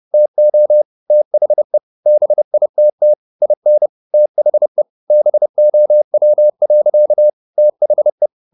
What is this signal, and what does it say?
Morse code, heard as a keyed tone, about 30 wpm, 600 Hz. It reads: TO THE BITT IN THE BOW. THE